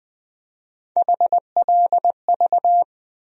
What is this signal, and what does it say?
Morse code, heard as a keyed tone, about 20 wpm, 700 Hz. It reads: HLV
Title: USA Call Sign Postfixes - 3 Letters 20wpm